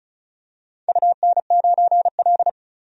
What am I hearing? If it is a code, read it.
Morse code, heard as a keyed tone, about 36 wpm, 700 Hz. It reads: UN9L
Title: DXCC-100 Call Signs - 4 Characters - 35wpm